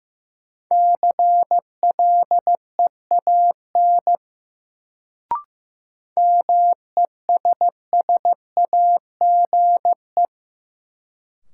Morse code, heard as a keyed tone, about 15 wpm, 700 Hz. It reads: CLEAN MESSAGE
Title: Sentences from Top 1100 Words - New Word Drill - 15wpm